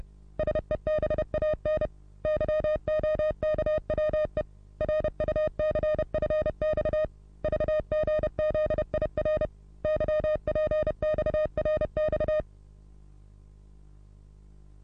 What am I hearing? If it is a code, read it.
Morse code, heard as a keyed tone, about 30 wpm, 620 Hz. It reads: SEBAN YOKWE RUCFX VGZIR YPXRX
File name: repos / Morsekurs / DJ4UF Morsen27